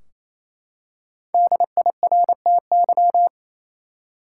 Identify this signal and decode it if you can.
Morse code, heard as a keyed tone, about 28 wpm, 700 Hz. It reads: DIRTY
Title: Five Letter Words - 28wpm